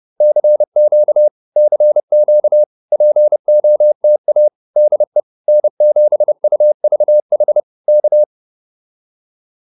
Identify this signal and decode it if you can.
Morse code, heard as a keyed tone, about 30 wpm, 600 Hz. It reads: CQ CQ POTA DE N7UVH K